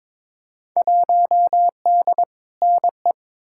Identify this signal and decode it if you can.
Morse code, heard as a keyed tone, about 22 wpm, 700 Hz. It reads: 1D NE
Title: Field Day Exchanges - 22wpm